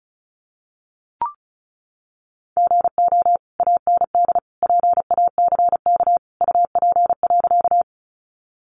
Morse code, heard as a keyed tone, about 34 wpm, 700 Hz. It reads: GO AND PACK UP.
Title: Sentences from Top 1300 Words 35wpm